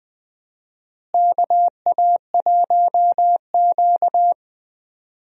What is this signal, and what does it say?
Morse code, heard as a keyed tone, about 20 wpm, 700 Hz. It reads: KA1Q
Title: USA Call Signs - 4 Characters - Repeated 2x - 20wpm